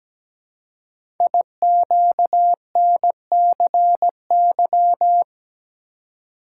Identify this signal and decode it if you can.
Morse code, heard as a keyed tone, about 17 wpm, 700 Hz. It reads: IQNCY
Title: Code Groups - 5 Letters - 17wpm